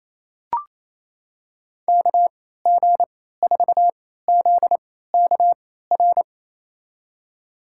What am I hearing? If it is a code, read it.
Morse code, heard as a keyed tone, about 28 wpm, spaced as for 16 wpm, 700 Hz. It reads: KG4ZKR
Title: USA Call Signs - 6 Characters - 3x Character Spacing - 28wpm